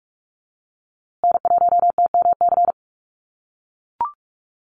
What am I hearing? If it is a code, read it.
Morse code, heard as a keyed tone, about 44 wpm, 700 Hz. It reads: N1TMC
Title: USA Call Signs Repeated 2x 45wpm